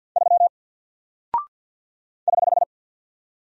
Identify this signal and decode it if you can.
Morse code, heard as a keyed tone, about 48 wpm, 700 Hz. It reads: 3 <HH>
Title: Single Character (Letter:Number:Punctuation:Prosign) Alt 50wpm